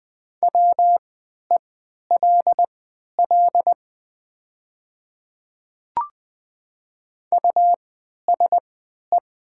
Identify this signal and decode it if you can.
Morse code, heard as a keyed tone, about 20 wpm, spaced as for 11 wpm, 700 Hz. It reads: WELL USE
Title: 3x Character Spacing - Top 200 Words - 20wpm